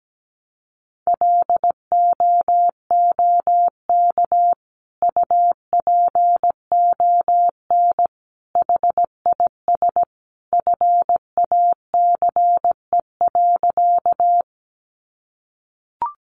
Morse code, heard as a keyed tone, about 17 wpm, 700 Hz. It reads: LOOK UPON HIS FACE.
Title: Sentences from Top 1100 Words 17wpm